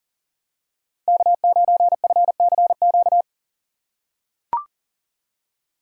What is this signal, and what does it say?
Morse code, heard as a keyed tone, about 40 wpm, 700 Hz. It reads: K9FCQ
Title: USA Call Signs - 5 Characters - 2x Character Spacing - 40wpm